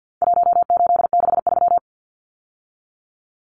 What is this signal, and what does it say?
Morse code, heard as a keyed tone, about 50 wpm, 700 Hz. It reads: 1863
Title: Numbers - Contest Style - 4-Digit 50wpm